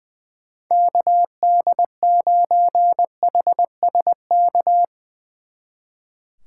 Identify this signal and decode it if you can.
Morse code, heard as a keyed tone, about 22 wpm, 700 Hz. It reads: KD9HSK
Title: Speed Racing - USA 2x3 Call Signs 20wpm